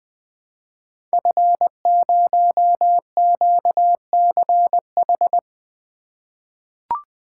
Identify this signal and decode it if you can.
Morse code, heard as a keyed tone, about 20 wpm, 700 Hz. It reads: F0QCH
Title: Code Groups - 5 Letter:Numbers - 20wpm